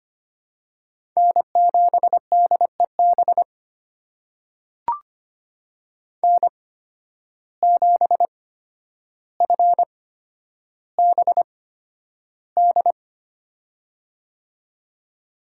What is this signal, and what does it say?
Morse code, heard as a keyed tone, about 25 wpm, spaced as for 11 wpm, 700 Hz. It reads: N7DEB N7FBD
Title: USA Call Signs - 5 Characters - 8x Character Spacing - 25wpm